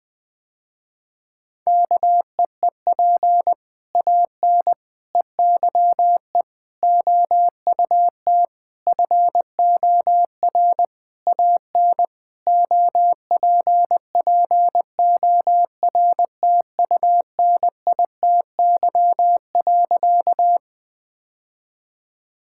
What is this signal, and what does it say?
Morse code, heard as a keyed tone, about 20 wpm, 700 Hz. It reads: KEEP AN EYE OUT FOR AN OPPORTUNITY.
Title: Sentences with 6-Word Phrases - Spoken Beginning - 20wpm